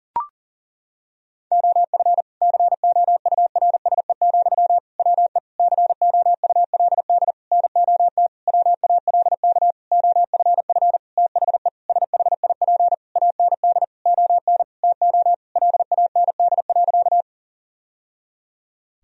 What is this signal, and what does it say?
Morse code, heard as a keyed tone, about 40 wpm, 700 Hz. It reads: OF COURSE, WE COULD NOT WALK OFF THE SHIP AND ON TO LAND.
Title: Sentences from Top 1000 Words - Review Part 2 - 40wpm